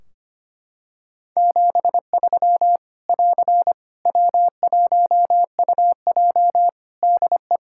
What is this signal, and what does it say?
Morse code, heard as a keyed tone, about 25 wpm, 700 Hz. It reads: 73 <AR> W1UJ DE
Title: Practice QSO - Exchange 4 - Ending - 25wpm